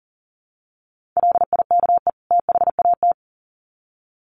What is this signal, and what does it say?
Morse code, heard as a keyed tone, about 37 wpm, 700 Hz. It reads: LIKE THAT
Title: Sets of 2 Words 40wpm